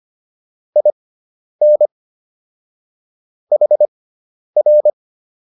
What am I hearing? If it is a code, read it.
Morse code, heard as a keyed tone, about 25 wpm, spaced as for 10 wpm, 600 Hz. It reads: IN HR